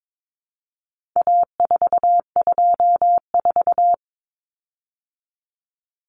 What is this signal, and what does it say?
Morse code, heard as a keyed tone, about 22 wpm, 700 Hz. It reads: A424
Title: Common Cut Numbers - 4-Digit 22wpm